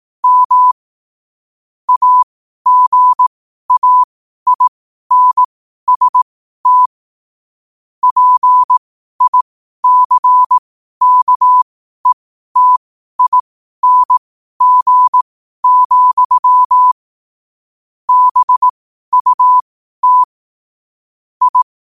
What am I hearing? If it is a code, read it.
Morse code, heard as a keyed tone, about 18 wpm, spaced as for 12 wpm, 1000 Hz. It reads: M AGAINST PICKETING, BUT I
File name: qotdfiles / 12wpm